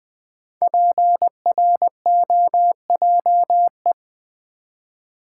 Morse code, heard as a keyed tone, about 20 wpm, 700 Hz. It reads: PROJE
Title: Build A Word - 5 Letters At A Time - 10-Letter Words - 20wpm